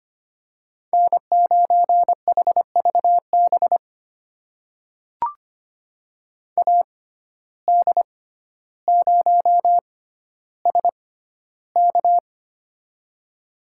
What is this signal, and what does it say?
Morse code, heard as a keyed tone, about 25 wpm, spaced as for 13 wpm, 700 Hz. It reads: N9HVB AD0SK
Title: USA Call Signs - 5 Characters - 6x Character Spacing - 25wpm